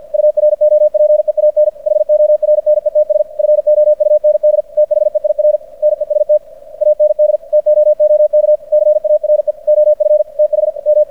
Morse code, heard as a keyed tone, about 50 wpm, 600 Hz. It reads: UROPEAN HORNETS FOUND THEIR NEST AND TOOK CARE OF TH M
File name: archived / CRSnet